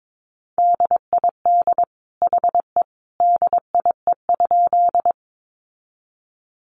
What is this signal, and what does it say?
Morse code, heard as a keyed tone, about 22 wpm, 700 Hz. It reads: DID HE DIE?
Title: Sentences from Top 1300 Words 22wpm